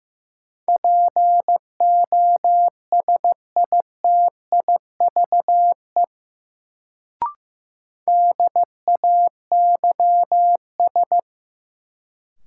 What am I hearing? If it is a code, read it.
Morse code, heard as a keyed tone, about 15 wpm, 700 Hz. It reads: POSITIVE DAYS
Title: Sentences from Top 1000 Words - Review Part 5 - New Word Drill - 15wpm